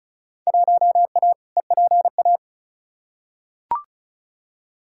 Morse code, heard as a keyed tone, about 36 wpm, 700 Hz. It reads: 1A EPA
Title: Field Day Exchanges - 35wpm